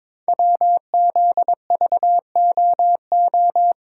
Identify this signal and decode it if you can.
Morse code, heard as a keyed tone, about 22 wpm, 700 Hz. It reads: WZVOO
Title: Code Groups - 5 Letter:Numbers - 22wpm